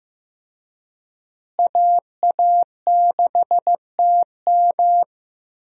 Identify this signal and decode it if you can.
Morse code, heard as a keyed tone, about 15 wpm, 700 Hz. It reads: AA6TM
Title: USA Call Signs - 5 Characters - Repeated 3x - 15wpm